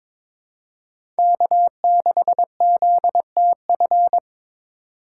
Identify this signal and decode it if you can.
Morse code, heard as a keyed tone, about 22 wpm, 700 Hz. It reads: K6ZTF
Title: USA Call Signs - 5 Characters - 4x Character Spacing - 22wpm